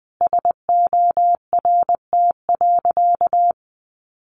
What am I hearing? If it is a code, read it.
Morse code, heard as a keyed tone, about 20 wpm, 700 Hz. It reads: SORT.
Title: Sentences from Top 1300 Words 20wpm